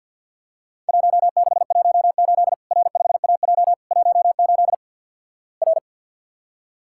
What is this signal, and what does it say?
Morse code, heard as a keyed tone, about 50 wpm, 700 Hz. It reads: 1618 R5AJ 18 R
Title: Worked All Europe DX Contest - QTC Exchange - 50wpm